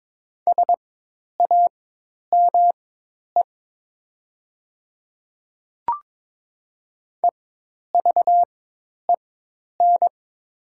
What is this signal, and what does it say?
Morse code, heard as a keyed tone, about 22 wpm, spaced as for 10 wpm, 700 Hz. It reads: SAME EVEN
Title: 4x Character Spacing - Top 200 Words - 22wpm